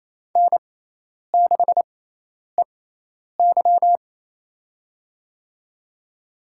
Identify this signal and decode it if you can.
Morse code, heard as a keyed tone, about 28 wpm, spaced as for 10 wpm, 700 Hz. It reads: N6EY
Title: USA Call Signs - 6x Character Spacing - 28wpm